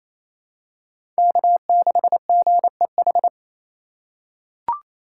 Morse code, heard as a keyed tone, about 28 wpm, 700 Hz. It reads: K6GEH
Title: USA Call Signs - 3x Character Spacing - 28wpm